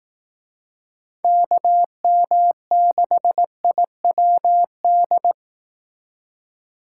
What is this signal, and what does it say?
Morse code, heard as a keyed tone, about 18 wpm, 700 Hz. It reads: KM6IWD
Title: Speed Racing - USA 2x3 Call Signs 15wpm